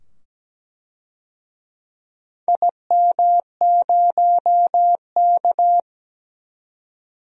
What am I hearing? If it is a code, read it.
Morse code, heard as a keyed tone, about 17 wpm, 700 Hz. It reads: IM0K
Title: DXCC-100 Call Signs - 4 Characters - 17wpm